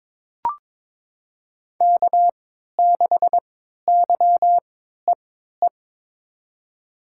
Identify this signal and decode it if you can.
Morse code, heard as a keyed tone, about 22 wpm, spaced as for 13 wpm, 700 Hz. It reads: K6YEE
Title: USA Call Signs - 5 Characters - 3x Character Spacing - 22wpm